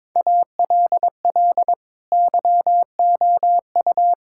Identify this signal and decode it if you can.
Morse code, heard as a keyed tone, about 22 wpm, 700 Hz. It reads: ALL YOU
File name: practice / 22wpm / Sets of 6 Words - 22wpm